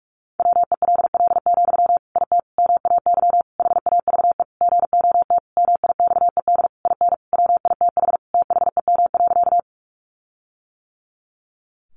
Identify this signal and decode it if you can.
Morse code, heard as a keyed tone, about 44 wpm, 700 Hz. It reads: WELL, IT MAY HAVE GOT MIXED IN WITH THEM.
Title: Sentences from Top 1500 Words 45wpm